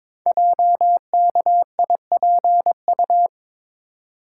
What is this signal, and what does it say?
Morse code, heard as a keyed tone, about 22 wpm, 690 Hz. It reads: JKIPU
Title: Code Groups - 5 Letters - 22wpm